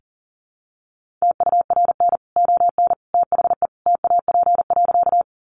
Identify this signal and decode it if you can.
Morse code, heard as a keyed tone, about 40 wpm, 700 Hz. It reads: TURN ON THE TAP.
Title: Sentences from Top 1400 Words 40wpm